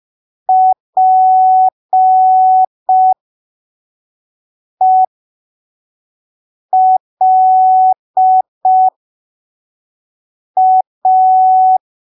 Morse code, heard as a keyed tone, about 5 wpm, spaced as for 3 wpm, 750 Hz. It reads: PELA
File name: arquivos / texto / texto 14 05ppm